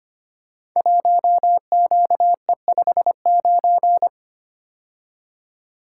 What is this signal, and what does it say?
Morse code, heard as a keyed tone, about 25 wpm, 700 Hz. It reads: 1QE59
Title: Code Groups - 5 Letter:Numbers - 25wpm